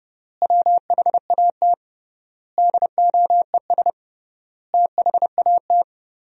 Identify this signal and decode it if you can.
Morse code, heard as a keyed tone, about 30 wpm, 700 Hz. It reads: WHAT DOES THAT
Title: 3x Word-Spacing - Sentences from Top 500 Words - 30wpm